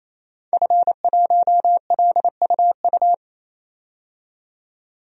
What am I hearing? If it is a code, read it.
Morse code, heard as a keyed tone, about 28 wpm, 700 Hz. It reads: F1LUU
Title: DXCC-100 Call Signs - 5 Characters - 28wpm